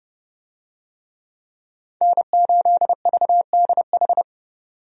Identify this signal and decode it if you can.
Morse code, heard as a keyed tone, about 32 wpm, 700 Hz. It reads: N8VDH
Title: USA Call Signs - 5 Characters - 30wpm